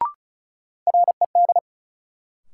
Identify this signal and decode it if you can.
Morse code, heard as a keyed tone, about 36 wpm, 700 Hz. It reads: RED
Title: Sentences from Top 700 Words - New Word Drill - 35wpm